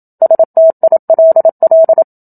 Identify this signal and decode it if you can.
Morse code, heard as a keyed tone, about 27 wpm, 650 Hz. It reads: STILL
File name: Book Two / 01-Chapter 1 - 27 WPM